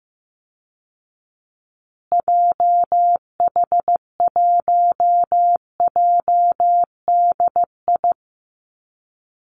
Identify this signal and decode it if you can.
Morse code, heard as a keyed tone, about 15 wpm, 700 Hz. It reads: JH1JDI
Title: Call Signs Repeated 2x - 15wpm